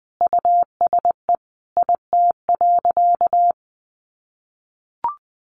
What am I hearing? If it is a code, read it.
Morse code, heard as a keyed tone, about 20 wpm, 700 Hz. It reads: USE IT.
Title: Sentences from Top 1100 Words 20wpm